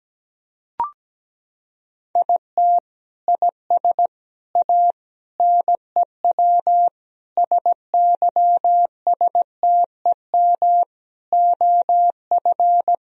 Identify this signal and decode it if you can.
Morse code, heard as a keyed tone, about 17 wpm, 700 Hz. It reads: IT IS A NEW SYSTEM OF
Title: Sentences from Top 1000 Words - Review Part 3 - 17wpm